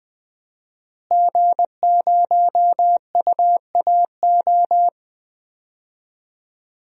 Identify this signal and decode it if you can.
Morse code, heard as a keyed tone, about 20 wpm, 700 Hz. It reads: G0UAO
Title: DXCC-100 Call Signs - 5 Characters - Repeated 2x - 20wpm